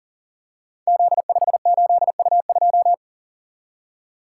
Morse code, heard as a keyed tone, about 42 wpm, 700 Hz. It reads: Z58U2
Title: Code Groups - 5 Letter:Numbers - 40wpm